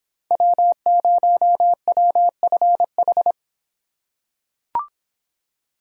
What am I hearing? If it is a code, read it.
Morse code, heard as a keyed tone, about 24 wpm, 700 Hz. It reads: W0WFH